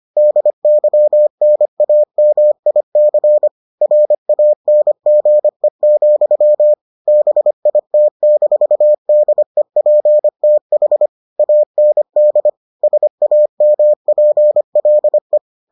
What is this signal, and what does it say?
Morse code, heard as a keyed tone, about 25 wpm, 600 Hz. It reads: DYNAMIC RANGE, BIT-DEPTH AND SAMPLE